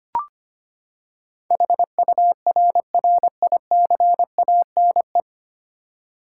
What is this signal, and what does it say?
Morse code, heard as a keyed tone, about 25 wpm, 700 Hz. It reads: HURRICANE
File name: Nine Letter Words - 25wpm